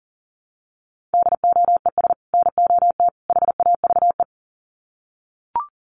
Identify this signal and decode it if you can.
Morse code, heard as a keyed tone, about 37 wpm, 700 Hz. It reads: DOES NOT HAVE
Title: Sets of 3 Words 40wpm